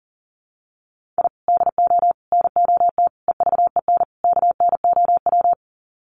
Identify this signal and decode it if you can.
Morse code, heard as a keyed tone, about 39 wpm, 700 Hz. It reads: I DO NOT EVEN KNOW
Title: Sets of 5 Words 40wpm